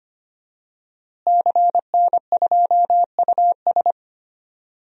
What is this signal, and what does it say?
Morse code, heard as a keyed tone, about 25 wpm, 700 Hz. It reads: CN2US
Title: DXCC-100 Call Signs - 5 Characters - Repeated 2x - 25wpm